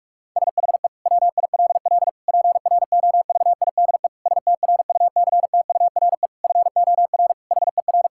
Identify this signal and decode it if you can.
Morse code, heard as a keyed tone, about 46 wpm, 700 Hz. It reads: SHE WILL PROVIDE STRUCTURE FOR HER
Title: Sentences from Top 1000 Words - Review Part 6 - 45wpm